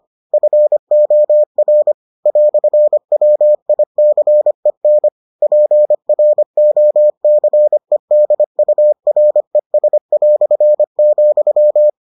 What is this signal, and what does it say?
Morse code, heard as a keyed tone, about 25 wpm, 600 Hz. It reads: FOR "WICEN PROCEDURES",